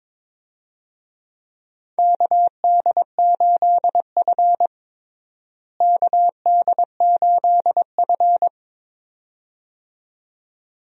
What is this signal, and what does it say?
Morse code, heard as a keyed tone, about 22 wpm, 700 Hz. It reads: KD8F KD8F
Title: USA Call Signs - 4 Characters - Repeated 2x - 22wpm